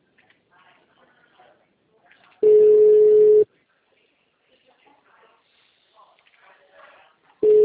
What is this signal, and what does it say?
Ringback tone (Europe, 425 Hz), with speech.
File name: out- -2011- - - .8104